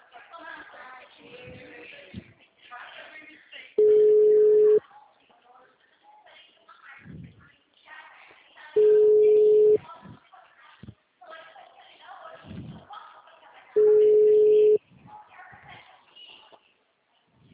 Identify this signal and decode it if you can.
Ringback tone (Europe, 425 Hz), with speech.